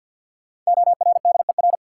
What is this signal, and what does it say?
Morse code, heard as a keyed tone, about 50 wpm, 700 Hz. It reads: ORDER
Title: Sentences from Top 400 Words - New Word Drill - 50wpm